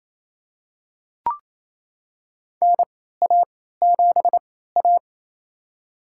Morse code, heard as a keyed tone, about 28 wpm, spaced as for 16 wpm, 700 Hz. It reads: NA7A
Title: USA Call Signs - 4 Characters - 3x Character Spacing - 28wpm